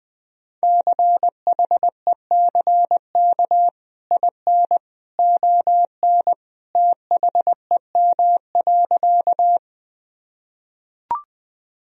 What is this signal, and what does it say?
Morse code, heard as a keyed tone, about 20 wpm, 700 Hz. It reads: CHECK IN ON THEM.
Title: Sentences from Top 1000 Words - Review Part 3 - 20wpm